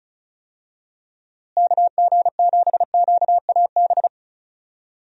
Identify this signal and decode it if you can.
Morse code, heard as a keyed tone, about 36 wpm, 700 Hz. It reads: KG7QAB
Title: USA Call Signs - 6 Characters - Repeated 3x - 35wpm